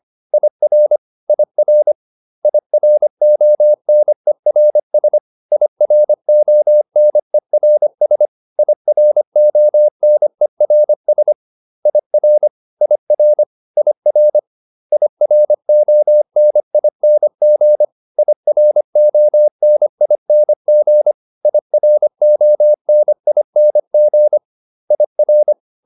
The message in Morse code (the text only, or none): IR IR IRONERS IRONERS IRONERS IR IR IR IRONING IRONING IRONING IR